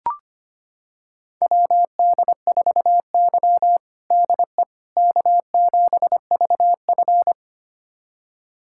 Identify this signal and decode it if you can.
Morse code, heard as a keyed tone, about 25 wpm, 700 Hz. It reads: WD4Y DE K7VF
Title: USA Callsign DE Callsign - 4 Characters - 25wpm